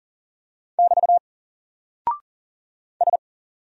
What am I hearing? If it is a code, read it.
Morse code, heard as a keyed tone, about 40 wpm, 700 Hz. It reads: <BT> S
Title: Single Character (Letter:Number:Punctuation:Prosign) 40wpm